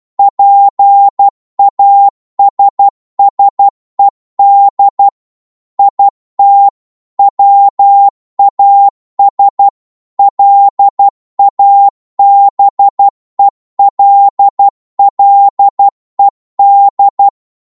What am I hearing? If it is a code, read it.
Morse code, heard as a keyed tone, about 12 wpm, 800 Hz. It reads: PASSED IT WAS LABELLED
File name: cwmp3 / psg-1-12wpm